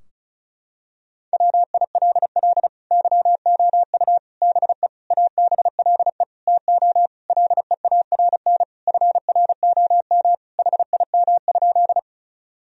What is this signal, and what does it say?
Morse code, heard as a keyed tone, about 36 wpm, 700 Hz. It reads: WILL YOU BE ABLE TO LEARN FROM HIM?
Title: Sentences from Top 1000 Words - Review Part 2 - 35wpm